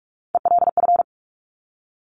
Morse code, heard as a keyed tone, about 46 wpm, 700 Hz. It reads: ELF
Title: Top Three Letter Combinations in Words 45wpm